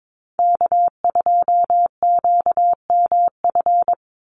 Morse code, heard as a keyed tone, about 22 wpm, 700 Hz. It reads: K2QMF
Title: Call Signs Repeated 2x - 22wpm